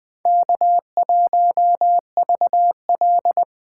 Morse code, heard as a keyed tone, about 20 wpm, 700 Hz. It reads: K1VL